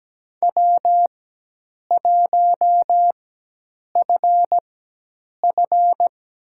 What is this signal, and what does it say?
Morse code, heard as a keyed tone, about 17 wpm, spaced as for 8 wpm, 700 Hz. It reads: W1FF
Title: USA Call Signs - 4 Characters - 4x Character Spacing - 17wpm